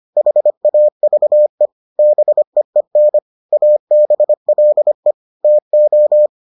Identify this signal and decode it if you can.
Morse code, heard as a keyed tone, about 25 wpm, 600 Hz. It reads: HAVE BEEN ABLE TO